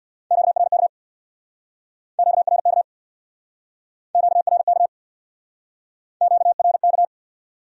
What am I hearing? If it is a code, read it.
Morse code, heard as a keyed tone, about 63 wpm, 700 Hz. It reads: QRK QRK QRK QRK